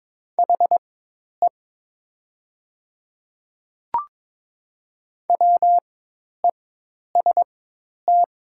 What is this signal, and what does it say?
Morse code, heard as a keyed tone, about 22 wpm, spaced as for 10 wpm, 700 Hz. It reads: HE WEST
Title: 4x Character Spacing - Top 500 Words - 22wpm